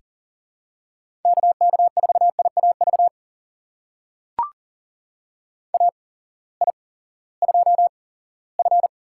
Morse code, heard as a keyed tone, about 39 wpm, spaced as for 20 wpm, 700 Hz. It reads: KK4IAV AI2F
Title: USA Call Signs - 8x Character Spacing - 40wpm